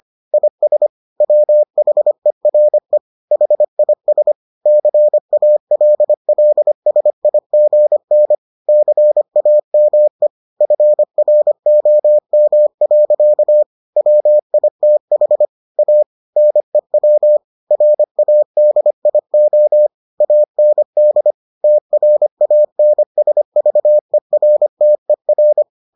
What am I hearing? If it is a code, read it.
Morse code, heard as a keyed tone, about 25 wpm, 600 Hz. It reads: IS WHERE HIS CALLSIGN CAME FROM. WITH A NEW RADIO AND TRANSVERTER